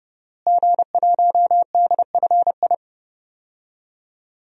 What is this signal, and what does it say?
Morse code, heard as a keyed tone, about 30 wpm, 700 Hz. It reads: G1DFI
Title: DXCC-100 Call Signs - 5 Characters - Repeated 3x - 30wpm